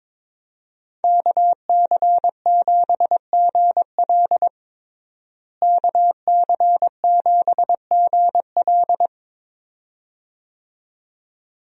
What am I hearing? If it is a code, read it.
Morse code, heard as a keyed tone, about 22 wpm, 700 Hz. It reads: KC7GL KC7GL
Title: USA Call Signs - 5 Characters - Repeated 2x - 22wpm